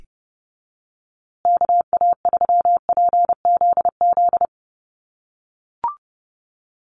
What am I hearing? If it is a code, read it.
Morse code, heard as a keyed tone, about 30 wpm, 700 Hz. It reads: KA3PZZ
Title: USA Call Signs Repeated 2x 30wpm